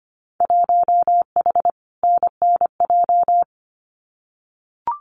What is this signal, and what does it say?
Morse code, heard as a keyed tone, about 25 wpm, 700 Hz. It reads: 1H NNJ
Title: Winter Field Day Exchanges 25wpm